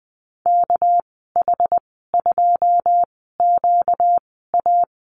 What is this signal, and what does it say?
Morse code, heard as a keyed tone, about 20 wpm, spaced as for 14 wpm, 700 Hz. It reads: KH2QA
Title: 245 Lesson 39g - Letter z - Call Signs 20wpm